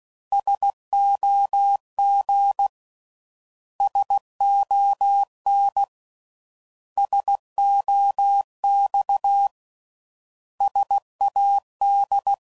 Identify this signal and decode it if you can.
Morse code, heard as a keyed tone, about 16 wpm, 800 Hz. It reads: SOG SON SOX SAD
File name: Prefixes / By First Letter / Ss